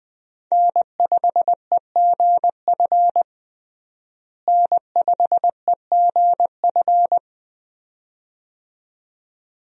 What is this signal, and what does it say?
Morse code, heard as a keyed tone, about 20 wpm, 700 Hz. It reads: N5EGF N5EGF
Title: USA Call Signs - 5 Characters - Repeated 2x - 20wpm